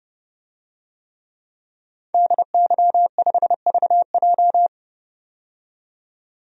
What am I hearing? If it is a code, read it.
Morse code, heard as a keyed tone, about 30 wpm, 700 Hz. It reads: DY5VJ
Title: DXCC-100 Call Signs - 5 Characters - 30wpm